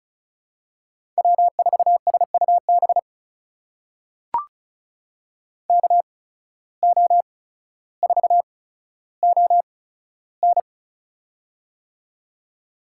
Morse code, heard as a keyed tone, about 34 wpm, spaced as for 15 wpm, 700 Hz. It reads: W4SUB KO4ON
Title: USA Call Signs - 5 Characters - 8x Character Spacing - 35wpm